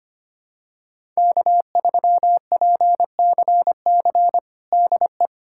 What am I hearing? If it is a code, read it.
Morse code, heard as a keyed tone, about 25 wpm, 700 Hz. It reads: K3PCC DE
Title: USA Callsign DE Callsign - 5 Characters - 25wpm